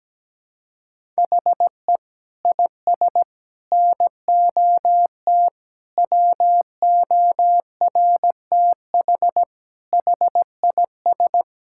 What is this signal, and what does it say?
Morse code, heard as a keyed tone, about 17 wpm, 700 Hz. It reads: HE IS NOT WORTH HIS
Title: Sentences from Top 1000 Words - Review Part 5 - 17wpm